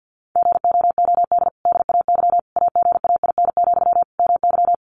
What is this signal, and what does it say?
Morse code, heard as a keyed tone, about 50 wpm, 700 Hz. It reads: GOOD DAY AGAIN, MY
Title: Sentences from Top 1100 Words 50wpm